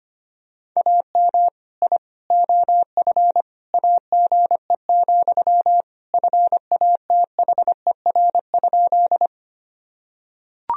Morse code, heard as a keyed tone, about 25 wpm, 700 Hz. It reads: AM I OF AGE, FATHER?
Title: Sentences from Top 1000 Words - Review Part 3 - 25wpm